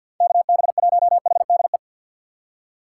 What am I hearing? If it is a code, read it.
Morse code, heard as a keyed tone, about 51 wpm, 700 Hz. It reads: KB1HDE
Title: USA Call Signs - 6 Characters - 6x Character Spacing - 50wpm